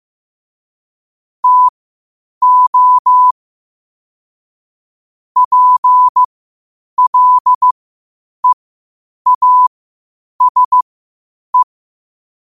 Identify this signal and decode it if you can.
Morse code, heard as a keyed tone, about 15 wpm, spaced as for 8 wpm, 1000 Hz. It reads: TO PLEASE
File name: qotdfiles / 8wpm